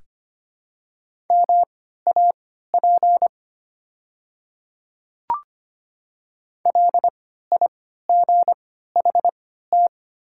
Morse code, heard as a keyed tone, about 25 wpm, spaced as for 14 wpm, 700 Hz. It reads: MAP LIGHT
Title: 3x Character Spacing - Top 500 Words - 25wpm